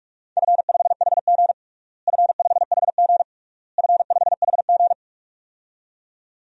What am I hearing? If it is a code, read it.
Morse code, heard as a keyed tone, about 46 wpm, 700 Hz. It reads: F5HG F5HG F5HG
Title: DXCC-100 Call Signs - 4 Characters - Repeated 3x - 45wpm